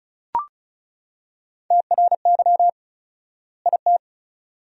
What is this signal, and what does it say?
Morse code, heard as a keyed tone, about 36 wpm, 700 Hz. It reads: TRY IT
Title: 4x Word-Spacing - Sentences from Top 400 Words - 35wpm